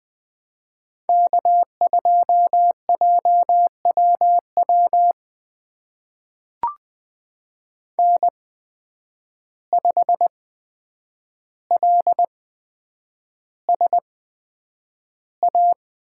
Morse code, heard as a keyed tone, about 20 wpm, spaced as for 9 wpm, 700 Hz. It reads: K2JWW N5LSA
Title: USA Call Signs - 5 Characters - 8x Character Spacing - 20wpm